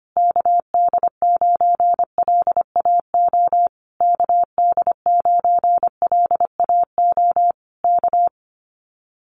Morse code, heard as a keyed tone, about 25 wpm, 700 Hz. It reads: KD9LAO KD9LAO K
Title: Common QSO Phrases Encore 25wpm